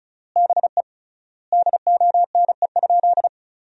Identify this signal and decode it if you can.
Morse code, heard as a keyed tone, about 36 wpm, 700 Hz. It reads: BE DONE?
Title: 3x Word-Spacing - Sentences from Top 500 Words - 35wpm